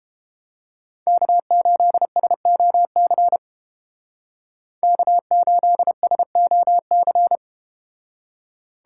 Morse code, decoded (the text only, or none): K8SOC K8SOC